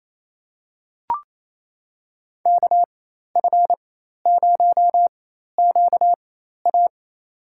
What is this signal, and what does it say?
Morse code, heard as a keyed tone, about 28 wpm, spaced as for 13 wpm, 700 Hz. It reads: KF0QA
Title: USA Call Signs - 5 Characters - 4x Character Spacing - 28wpm